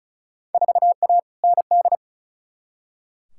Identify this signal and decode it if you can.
Morse code, heard as a keyed tone, about 36 wpm, 690 Hz. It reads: 4A ND
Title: Field Day Exchanges Encore - 35wpm